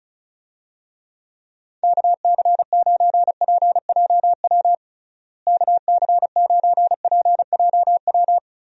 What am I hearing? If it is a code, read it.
Morse code, heard as a keyed tone, about 36 wpm, 700 Hz. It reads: KC9PJW KC9PJW
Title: USA Call Signs - 6 Characters - Repeated 3x - 35wpm